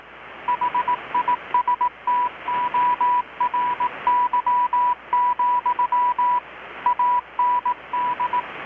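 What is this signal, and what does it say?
Morse code, heard as a keyed tone, about 18 wpm, 980 Hz, with radio noise. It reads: HISTORY, AND
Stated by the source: Morse from KPH's Night of Nights XX maritime radiotelegraphy event, received in Chiapas, Mexico about 3,800 km away on a Kenwood R-600 with a grounded 10 m long-wire antenna.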